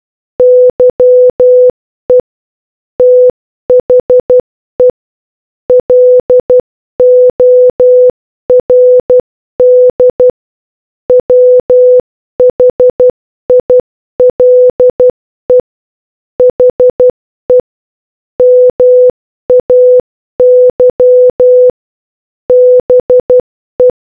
Morse code, heard as a keyed tone, about 12 wpm, 500 Hz. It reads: YE THE LORD WHILE HE MAY BE